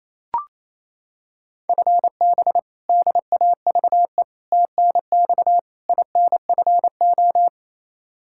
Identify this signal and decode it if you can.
Morse code, heard as a keyed tone, about 28 wpm, 700 Hz. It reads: FB DAVE TNX INFO
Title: K1USN SST QSO Practice 28wpm